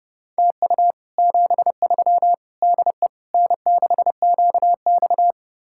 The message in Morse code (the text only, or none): TU 73 DE N6QX